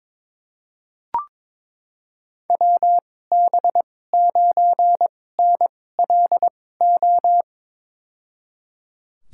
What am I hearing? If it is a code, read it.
Morse code, heard as a keyed tone, about 22 wpm, spaced as for 16 wpm, 700 Hz. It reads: WB9NLO
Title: USA Call Signs - 2x Character Spacing - 22wpm